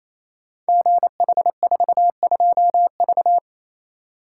Morse code, heard as a keyed tone, about 29 wpm, 700 Hz. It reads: GH42V
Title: Code Groups - 5 Letter:Numbers - 28wpm